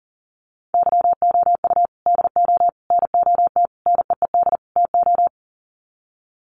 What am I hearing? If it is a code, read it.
Morse code, heard as a keyed tone, about 40 wpm, 700 Hz. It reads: YOU DO NOT NEED TO
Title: Sets of 5 Words 40wpm